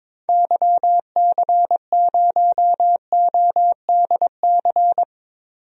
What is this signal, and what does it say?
Morse code, heard as a keyed tone, about 22 wpm, 700 Hz. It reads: YC0ODC
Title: DXCC-100 Call Signs - 6 Characters - 22wpm